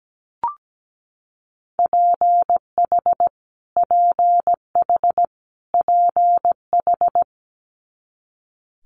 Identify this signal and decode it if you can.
Morse code, heard as a keyed tone, about 17 wpm, 700 Hz. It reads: PH PH PH
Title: Multi-letter Phonemes Repeated 3x 17wpm